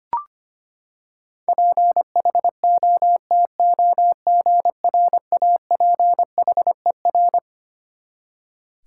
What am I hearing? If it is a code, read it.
Morse code, heard as a keyed tone, about 25 wpm, 700 Hz. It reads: PHOTOGRAPHER
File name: Twelve Letter Words - 25wpm